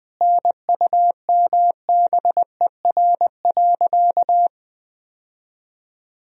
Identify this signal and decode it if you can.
Morse code, heard as a keyed tone, about 20 wpm, 700 Hz. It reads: NUMBER.
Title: Sentences from Top 1000 Words - Review Part 1 - 20wpm